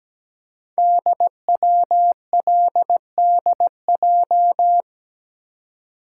Morse code, heard as a keyed tone, about 17 wpm, 700 Hz. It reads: DWLDJ
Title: Code Groups - 5 Letters - 17wpm